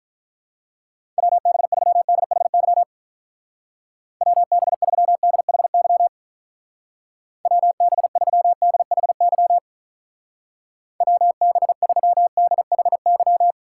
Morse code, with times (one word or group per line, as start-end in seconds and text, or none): 1.2-2.8 WB3DHY
4.2-6.1 WB3DHY
7.5-9.6 WB3DHY
11.0-13.5 WB3DHY